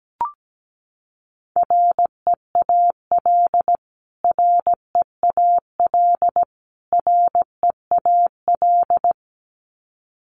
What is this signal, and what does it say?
Morse code, heard as a keyed tone, about 17 wpm, 700 Hz. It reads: REAL REAL REAL
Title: Top 300 Words Repeated 17wpm